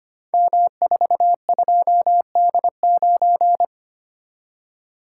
Morse code, heard as a keyed tone, about 25 wpm, 700 Hz. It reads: M42D9
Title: Code Groups - 5 Letter:Numbers - 25wpm